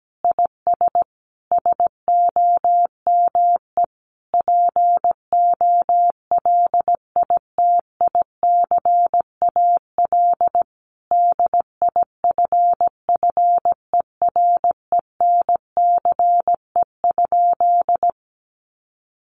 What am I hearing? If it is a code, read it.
Morse code, heard as a keyed tone, about 17 wpm, 700 Hz. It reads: IS SOME POLITICAL DIFFERENCE?
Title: Sentences from Top 800 Words 17wpm